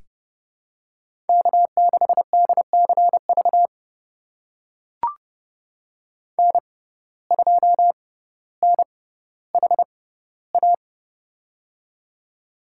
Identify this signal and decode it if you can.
Morse code, heard as a keyed tone, about 31 wpm, spaced as for 15 wpm, 700 Hz. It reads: K6DCV N2NHA
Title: USA Call Signs - 5 Characters - 6x Character Spacing - 30wpm